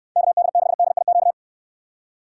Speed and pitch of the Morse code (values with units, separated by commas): 61 wpm, 700 Hz